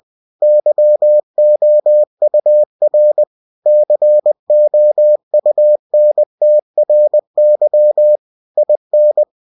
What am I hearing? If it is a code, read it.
Morse code, heard as a keyed tone, about 20 wpm, 600 Hz. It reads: YOUR COUNTRY IN
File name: cw-books / winnie-the-pooh / winnie-20wpm-page-0114